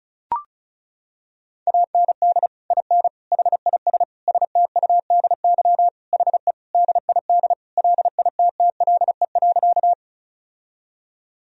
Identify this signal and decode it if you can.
Morse code, heard as a keyed tone, about 34 wpm, 700 Hz. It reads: AND IN HIS STUDY HE DID LITTLE.
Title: Sentences from Top 1000 Words - Review Part 1 - 35wpm